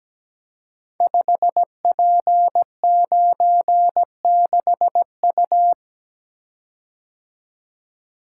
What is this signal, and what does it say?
Morse code, heard as a keyed tone, about 17 wpm, 700 Hz. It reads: 5P96U
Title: Code Groups - 5 Letter:Numbers - 17wpm